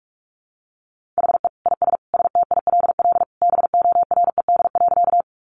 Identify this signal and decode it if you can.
Morse code, heard as a keyed tone, about 44 wpm, 700 Hz. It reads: HE IS STILL BORED.
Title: Easy Phrases 45wpm